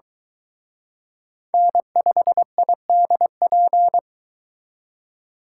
Morse code, heard as a keyed tone, about 22 wpm, 700 Hz. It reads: N5IDP